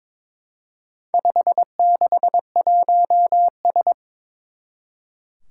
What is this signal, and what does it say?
Morse code, heard as a keyed tone, about 22 wpm, 700 Hz. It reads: 561S
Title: Four Challenging Letter-Number Combinations 22wpm